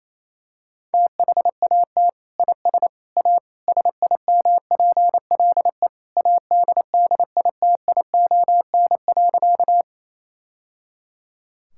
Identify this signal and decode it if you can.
Morse code, heard as a keyed tone, about 28 wpm, 700 Hz. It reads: THAT IS A SIMPLE ADDITION.
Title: Sentences from Top 1000 Words - Review Part 6 - 28wpm